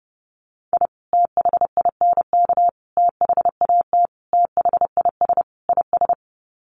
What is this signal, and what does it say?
Morse code, heard as a keyed tone, about 30 wpm, 700 Hz. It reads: I THINK THAT THIS IS
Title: Sets of 5 Words Encore2 30wpm